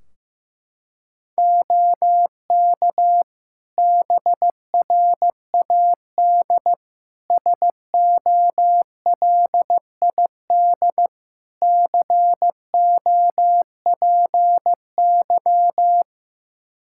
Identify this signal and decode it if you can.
Morse code, heard as a keyed tone, about 15 wpm, 700 Hz. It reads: OK BRAD SOLID COPY
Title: K1USN SST QSO Practice 15wpm